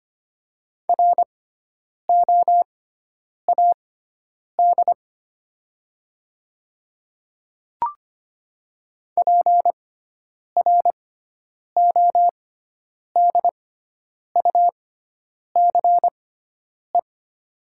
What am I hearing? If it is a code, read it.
Morse code, heard as a keyed tone, about 25 wpm, spaced as for 9 wpm, 700 Hz. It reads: ROAD PRODUCE
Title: 6x Character Spacing - Top 500 Words - 25wpm